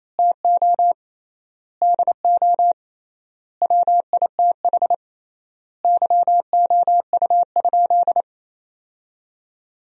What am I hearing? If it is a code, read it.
Morse code, heard as a keyed tone, about 27 wpm, 700 Hz. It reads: TO DO WITH YOU?
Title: 3x Word-Spacing - Sentences from Top 100 Words - 28wpm